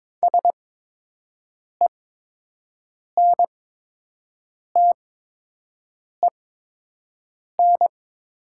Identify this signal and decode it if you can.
Morse code, heard as a keyed tone, about 22 wpm, spaced as for 6 wpm, 700 Hz. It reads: SENTEN
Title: 8x Character Spacing - Top 300 Words - 22wpm